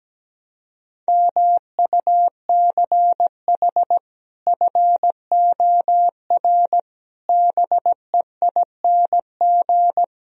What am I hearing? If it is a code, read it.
Morse code, heard as a keyed tone, about 17 wpm, 700 Hz. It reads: MUCH FOR BEING
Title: Speed-Racing - Sets of 5 Words 17wpm